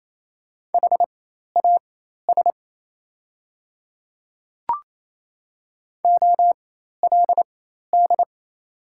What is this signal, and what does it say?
Morse code, heard as a keyed tone, about 28 wpm, spaced as for 13 wpm, 700 Hz. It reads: HAS OLD
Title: 4x Character Spacing - Top 300 Words - 28wpm